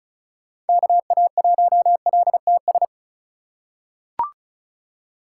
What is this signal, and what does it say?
Morse code, heard as a keyed tone, about 34 wpm, 700 Hz. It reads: KA1LTS
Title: USA Call Signs - 6 Characters - 2x Character Spacing - 35wpm